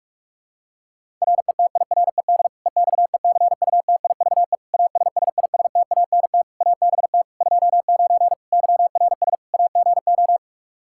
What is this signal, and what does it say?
Morse code, heard as a keyed tone, about 41 wpm, 700 Hz. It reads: RETIRED EXECUTIVE ASSISTANT ABT 29 YRS AGO
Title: Practice QSO - Exchange 3 - More Info - 45wpm